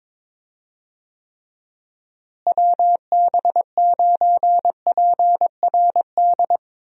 Morse code, heard as a keyed tone, about 22 wpm, 700 Hz. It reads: WB9PRD
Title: USA Call Signs - 6 Characters - Repeated 2x - 22wpm